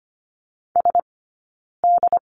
Morse code, heard as a keyed tone, about 25 wpm, 700 Hz. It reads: S D
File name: Lesson 15d - Number 5 - Two Characters 25wpm